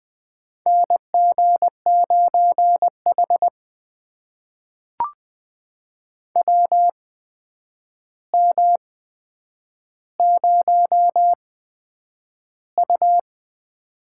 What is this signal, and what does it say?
Morse code, heard as a keyed tone, about 20 wpm, spaced as for 9 wpm, 700 Hz. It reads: NG9H WM0U
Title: USA Call Signs - 4 Characters - 8x Character Spacing - 20wpm